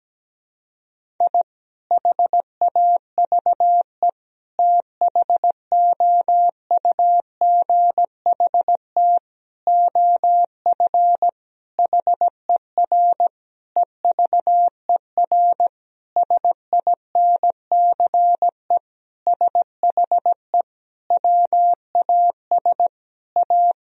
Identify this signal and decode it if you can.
Morse code, heard as a keyed tone, about 17 wpm, 700 Hz. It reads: I HAVE THOUGHT OF HER EVER SINCE SHE WAS A
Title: Sentences from Top 1000 Words - Review Part 1 - 17wpm